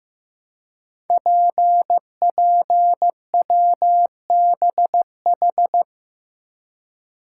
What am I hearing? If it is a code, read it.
Morse code, heard as a keyed tone, about 15 wpm, 700 Hz. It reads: PPWBH
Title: Code Groups - 5 Letters - 15wpm